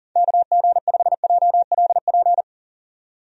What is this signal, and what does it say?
Morse code, heard as a keyed tone, about 40 wpm, 700 Hz. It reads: KG5JLP
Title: USA Call Signs - 6 Characters - 40wpm